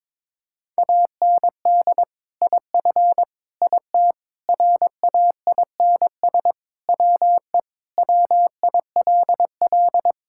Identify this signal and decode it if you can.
Morse code, heard as a keyed tone, about 22 wpm, 700 Hz. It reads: AND IF IT RAINS WE WILL
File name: practice / 22wpm / Sentences with 2-Word Phrases - 22wpm